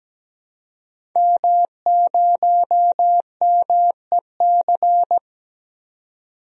Morse code, heard as a keyed tone, about 17 wpm, 700 Hz. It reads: M0MEC
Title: DXCC-100 Call Signs - 5 Characters - 17wpm